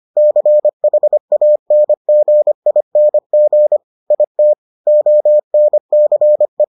Morse code, heard as a keyed tone, about 25 wpm, 600 Hz. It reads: CHANGING IT ONCE